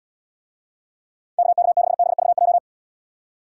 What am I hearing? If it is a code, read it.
Morse code, heard as a keyed tone, about 78 wpm, 700 Hz. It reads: KK6BVJ